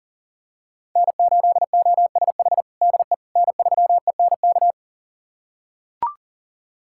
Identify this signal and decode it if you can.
Morse code, heard as a keyed tone, about 40 wpm, 700 Hz. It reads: N8OSH DE N3ENK
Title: USA Callsign DE Callsign - 5 Characters - 40wpm